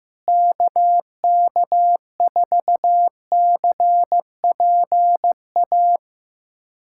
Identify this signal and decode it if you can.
Morse code, heard as a keyed tone, about 15 wpm, 700 Hz. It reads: KK4CPA
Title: USA Callsign DE Callsign 15wpm